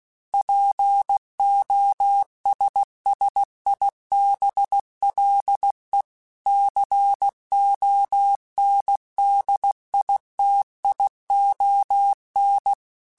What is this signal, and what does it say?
Morse code, heard as a keyed tone, about 16 wpm, 800 Hz. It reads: POSSIBLE CONDITION
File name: Technical Subjects / Technical Subjects-mp3s / AND Gate @15